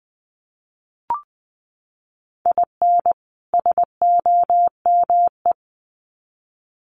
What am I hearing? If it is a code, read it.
Morse code, heard as a keyed tone, about 20 wpm, 700 Hz. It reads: IN SOME
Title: Sets of 2 Words 20wpm